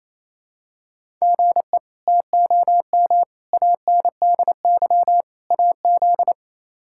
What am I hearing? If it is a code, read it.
Morse code, heard as a keyed tone, about 28 wpm, 700 Hz. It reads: GE TOM ANDY AZ
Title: K1USN SST QSO Practice - 28wpm